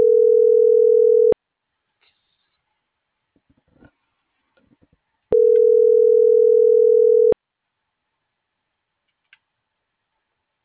Ringback tone (North America), 440 + 480 Hz.